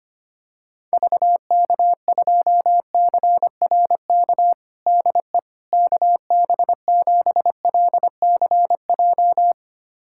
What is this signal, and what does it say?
Morse code, heard as a keyed tone, about 25 wpm, 700 Hz. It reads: VK2CRK DE KB7LCJ
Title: USA Callsign DE Callsign 25wpm